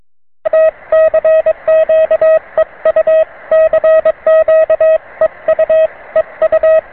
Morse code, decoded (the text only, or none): T CQ EU CQ EU EU